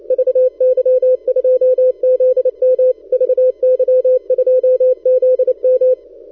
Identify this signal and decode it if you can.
Morse code, heard as a keyed tone, about 28 wpm, 510 Hz. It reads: VY2ZM VY2ZM
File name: pa0oos / recall2013 / vy2zm 160test